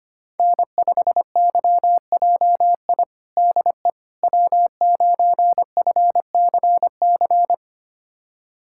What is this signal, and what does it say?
Morse code, heard as a keyed tone, about 25 wpm, 700 Hz. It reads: N5YJI DE W9FCC
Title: USA Callsign DE Callsign - 5 Characters - 25wpm